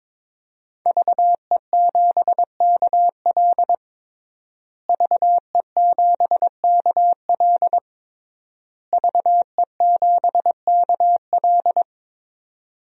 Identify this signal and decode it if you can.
Morse code, heard as a keyed tone, about 22 wpm, 700 Hz. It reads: VE7KL VE7KL VE7KL
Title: DXCC-100 Call Signs - 5 Characters - Repeated 3x - 22wpm